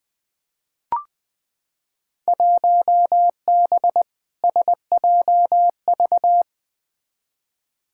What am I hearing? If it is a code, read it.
Morse code, heard as a keyed tone, about 21 wpm, 700 Hz. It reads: 1B SJV
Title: Field Day Exchanges - 20wpm